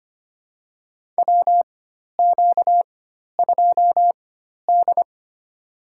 Morse code, heard as a keyed tone, about 25 wpm, spaced as for 12 wpm, 700 Hz. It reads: WQ2D
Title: USA Call Signs - 4x Character Spacing - 25wpm